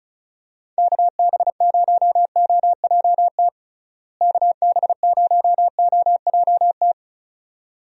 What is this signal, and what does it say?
Morse code, heard as a keyed tone, about 36 wpm, 700 Hz. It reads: KB0OJT KB0OJT
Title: USA Call Signs - 6 Characters - Repeated 2x - 35wpm